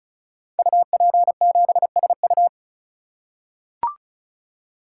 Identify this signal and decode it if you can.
Morse code, heard as a keyed tone, about 36 wpm, 700 Hz. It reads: UP7SU
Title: Code Groups - 5 Letter:Numbers - 35wpm